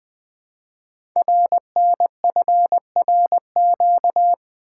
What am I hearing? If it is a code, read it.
Morse code, heard as a keyed tone, about 20 wpm, 700 Hz. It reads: RNFRQ
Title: Code Groups - 5 Letters - 20wpm